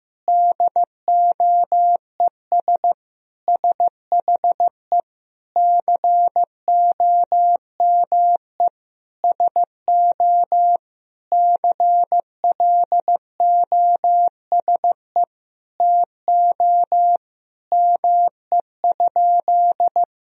Morse code, heard as a keyed tone, about 15 wpm, 700 Hz. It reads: DOES SHE COME SO CLOSE TO ME?
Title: Sentences from Top 1000 Words - Review Part 1 - 15wpm